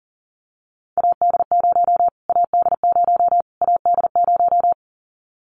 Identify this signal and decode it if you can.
Morse code, heard as a keyed tone, about 40 wpm, 700 Hz. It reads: AD0 AD0 AD0
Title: Callsign Prefixes for Top 25 DXCC - Repeated 3x - 40wpm